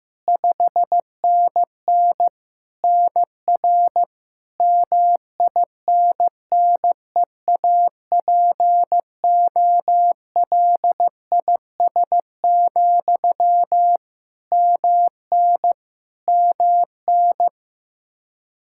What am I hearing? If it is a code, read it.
Morse code, heard as a keyed tone, about 16 wpm, 700 Hz. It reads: 5NN NR MINNEAPOLIS, MN MN
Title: Practice QSO - Exchange 1 - Beginning - 15wpm